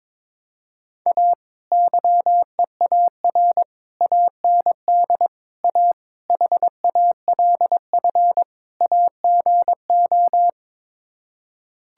Morse code, heard as a keyed tone, about 22 wpm, 700 Hz. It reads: A YEAR AND A HALF AGO
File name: practice / 22wpm / Sets of 6 Words - 22wpm